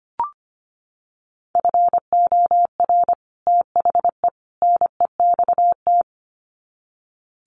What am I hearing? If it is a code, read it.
Morse code, heard as a keyed tone, about 25 wpm, 700 Hz. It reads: FOR THE NEXT
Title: Sets of 3 Words 25wpm